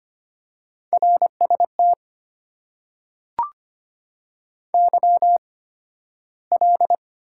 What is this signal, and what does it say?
Morse code, heard as a keyed tone, about 25 wpm, spaced as for 13 wpm, 690 Hz. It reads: RST YL
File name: Top 90 QSO Elements - 8x Character Spacing - 25wpm